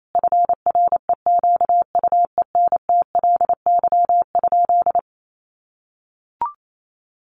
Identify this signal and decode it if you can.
Morse code, heard as a keyed tone, about 28 wpm, 700 Hz. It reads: FREQUENTLY?
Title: Sentences from Top 1300 Words 28wpm